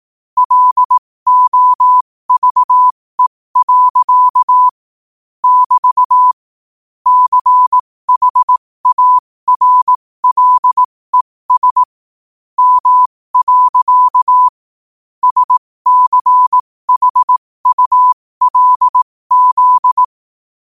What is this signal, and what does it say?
Morse code, heard as a keyed tone, about 18 wpm, spaced as for 15 wpm, 1000 Hz. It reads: LOVE. <BT> CHARLES M. SCHULZ